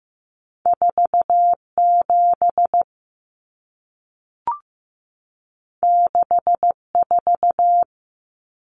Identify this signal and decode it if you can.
Morse code, heard as a keyed tone, about 15 wpm, 700 Hz. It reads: 47 64
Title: Numbers - 2-Digit 15wpm